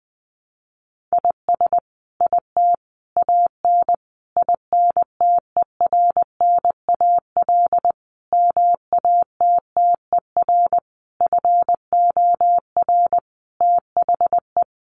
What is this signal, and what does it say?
Morse code, heard as a keyed tone, about 20 wpm, 700 Hz. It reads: IS IT AN INTERNAL MATTER FOR THE
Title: Sentences from Top 1400 Words 20wpm